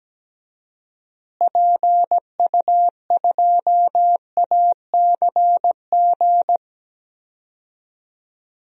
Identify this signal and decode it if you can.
Morse code, heard as a keyed tone, about 17 wpm, 700 Hz. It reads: PU2ACG
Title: DXCC-100 Call Signs - 6 Characters - Repeated 2x - 17wpm